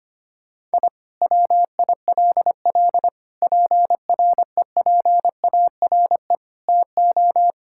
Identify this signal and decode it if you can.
Morse code, heard as a keyed tone, about 25 wpm, 700 Hz. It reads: I WILL PREPARE TO
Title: Sentences from Top 1000 Words - Review Part 6 - 25wpm